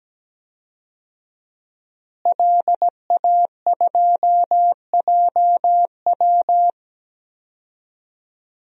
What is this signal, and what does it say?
Morse code, heard as a keyed tone, about 17 wpm, 700 Hz. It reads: LA2JW
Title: DXCC-100 Call Signs - 5 Characters - Repeated 2x - 17wpm